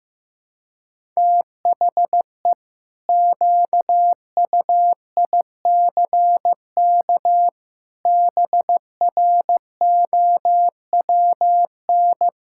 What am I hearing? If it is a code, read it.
Morse code, heard as a keyed tone, about 16 wpm, 700 Hz. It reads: THE QUICK BROWN
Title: Pangram Challenge v2 - 15wpm